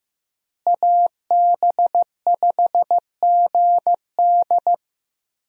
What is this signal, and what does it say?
Morse code, heard as a keyed tone, about 15 wpm, 700 Hz. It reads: AB5GD
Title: USA Call Signs - 5 Characters - Repeated 2x - 15wpm